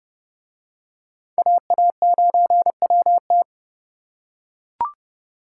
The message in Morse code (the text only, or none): AA9WT